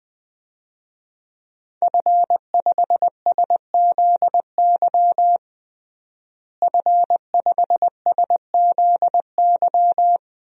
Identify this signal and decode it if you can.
Morse code, heard as a keyed tone, about 21 wpm, 700 Hz. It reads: F5SZY F5SZY
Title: DXCC-100 Call Signs - 5 Characters - Repeated 3x - 20wpm